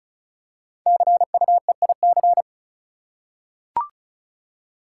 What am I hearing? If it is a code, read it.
Morse code, heard as a keyed tone, about 36 wpm, 690 Hz. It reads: CUEIC
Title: Code Groups - 5 Letter:Numbers - 35wpm